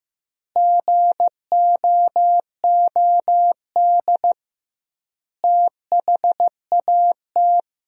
Morse code, heard as a keyed tone, about 15 wpm, 700 Hz. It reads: GOOD THAT
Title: 2x Word-Spacing - Sentences from Top 200 Words - 15wpm